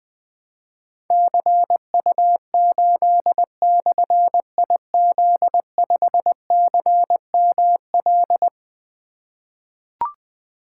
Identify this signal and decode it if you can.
Morse code, heard as a keyed tone, about 20 wpm, 700 Hz. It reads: CU8/IZ5CML
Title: Difficult Call Signs 20wpm